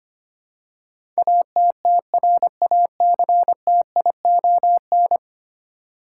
Morse code, heard as a keyed tone, about 25 wpm, 700 Hz. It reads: ATTRACTION
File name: 10 Letter Words - First 3 Letters Given - 25wpm